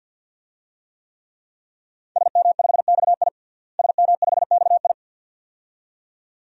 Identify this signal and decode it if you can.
Morse code, heard as a keyed tone, about 47 wpm, 700 Hz. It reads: SM5XI SM5XI
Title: DXCC-100 Call Signs - 5 Characters - Repeated 2x - 50wpm